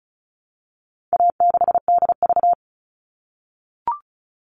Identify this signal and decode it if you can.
Morse code, heard as a keyed tone, about 34 wpm, 700 Hz. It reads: A6DV
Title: All Cut Numbers - 4-Digit 35wpm